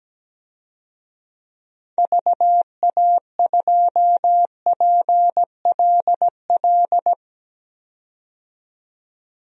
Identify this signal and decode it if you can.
Morse code, heard as a keyed tone, about 17 wpm, 700 Hz. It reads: VA2PLL
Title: DXCC-100 Call Signs - 6 Characters - 17wpm